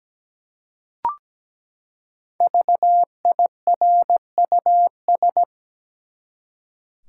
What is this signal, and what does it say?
Morse code, heard as a keyed tone, about 17 wpm, 700 Hz. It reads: VIRUS
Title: Five Letter Words - 17wpm